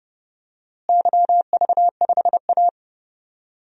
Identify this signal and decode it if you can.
Morse code, heard as a keyed tone, about 32 wpm, 700 Hz. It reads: YV5A
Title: DXCC-100 Call Signs - 4 Characters - Repeated 2x - 30wpm